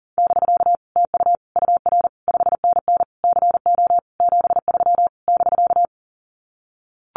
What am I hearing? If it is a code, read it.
Morse code, heard as a keyed tone, about 40 wpm, 700 Hz. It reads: <BK> TU UR 5NN CO 73 <BK>
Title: Common QSO Phrases Encore 40wpm